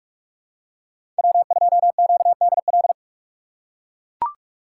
Morse code, heard as a keyed tone, about 44 wpm, 700 Hz. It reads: W2QDL
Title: USA Call Signs - 4x Character Spacing - 45wpm